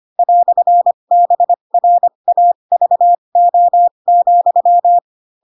Morse code, heard as a keyed tone, about 25 wpm, spaced as for 22 wpm, 700 Hz. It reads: "BRAVO,
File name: Pinocchio / Pinocchio ch2 18 25 ik1hge